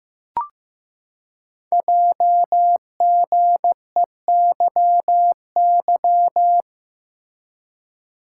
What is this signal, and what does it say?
Morse code, heard as a keyed tone, about 15 wpm, 700 Hz. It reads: JGEYY
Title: Code Groups - 5 Letters - 15wpm